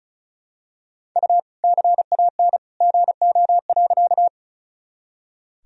Morse code, heard as a keyed tone, about 36 wpm, 700 Hz. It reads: U CAN GO.
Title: Sentences from Top 100 Words and Abbreviations 35wpm